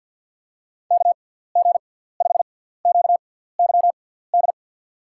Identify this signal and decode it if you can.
Morse code, heard as a keyed tone, about 53 wpm, spaced as for 17 wpm, 700 Hz. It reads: KG5QYD
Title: USA Call Signs - 6 Characters - 6x Character Spacing - 50wpm